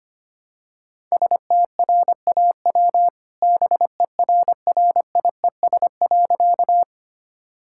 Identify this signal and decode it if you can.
Morse code, heard as a keyed tone, about 25 wpm, 700 Hz. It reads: STRAW BERRIES.
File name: practice / 25wpm / All-Age Appropriate Jokes - 25wpm